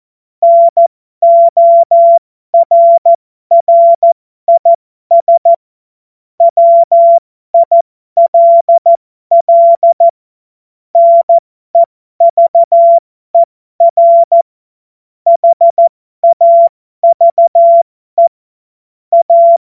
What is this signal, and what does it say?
Morse code, heard as a keyed tone, about 14 wpm, spaced as for 12 wpm, 680 Hz. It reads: NORRIS WILL NEVER HAVE A
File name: Morsekurs / Witze